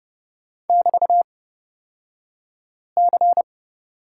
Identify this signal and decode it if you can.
Morse code, heard as a keyed tone, about 31 wpm, spaced as for 5 wpm, 700 Hz. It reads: <BT>C
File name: Lesson 36d - BT - Two Characters 30wpm